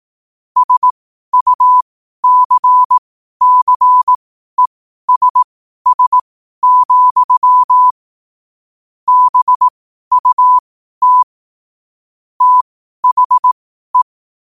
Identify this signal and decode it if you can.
Morse code, heard as a keyed tone, about 18 wpm, spaced as for 12 wpm, 1000 Hz. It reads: SUCCESS, BUT THE
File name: qotdfiles / 12wpm